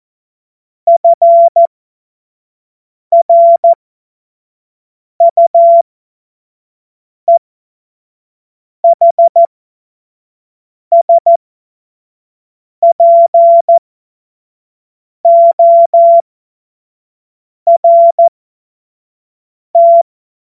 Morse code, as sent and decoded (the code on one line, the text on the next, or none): ..-. .-. ..- . .... ... .--. --- .-. -
FRUEHSPORT